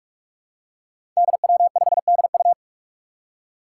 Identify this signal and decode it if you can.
Morse code, heard as a keyed tone, about 46 wpm, 700 Hz. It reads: DW5DU
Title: Five Challenging Letter-Number Combinations 45wpm